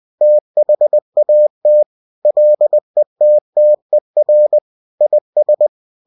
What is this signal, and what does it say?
Morse code, heard as a keyed tone, about 20 wpm, 600 Hz. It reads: THAT LETTER IS